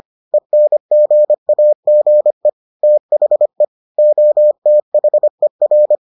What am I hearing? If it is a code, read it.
Morse code, heard as a keyed tone, about 25 wpm, 600 Hz. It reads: ENGAGE THE OTHER